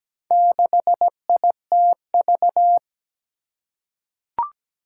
Morse code, heard as a keyed tone, about 17 wpm, 700 Hz. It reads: 6ITV
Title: Four Challenging Letter-Number Combinations 17wpm